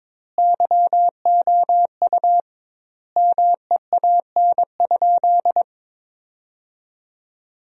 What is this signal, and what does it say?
Morse code, heard as a keyed tone, about 22 wpm, 700 Hz. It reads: YOU MEAN?
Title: 2x Word-Spacing - Sentences from Top 200 Words - 22wpm